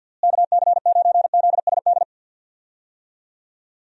Morse code, heard as a keyed tone, about 51 wpm, 700 Hz. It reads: KC9ZSD
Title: USA Call Signs - 6 Characters - Repeated 3x - 50wpm